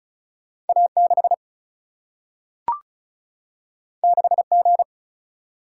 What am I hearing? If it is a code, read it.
Morse code, heard as a keyed tone, about 36 wpm, 700 Hz. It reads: A 6 6 G
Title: Single-Letter Single-Number 35wpm